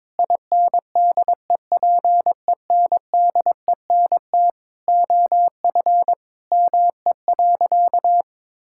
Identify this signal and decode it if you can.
Morse code, heard as a keyed tone, about 22 wpm, 700 Hz. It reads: INDEPENDENT OF ME.
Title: Sentences from Top 1000 Words - Review Part 6 - 22wpm